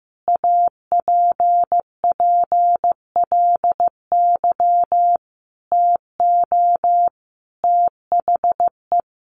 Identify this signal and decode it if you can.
Morse code, heard as a keyed tone, about 14 wpm, 700 Hz. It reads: APPLY TO THE
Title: Sentences from Top 1200 Words 15wpm